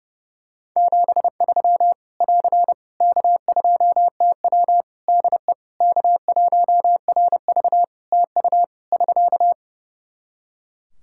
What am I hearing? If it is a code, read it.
Morse code, heard as a keyed tone, about 30 wpm, 700 Hz. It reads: 73 <AR> K2TW DE K1RV TU <SK>
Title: K1USN SST QSO Practice 30wpm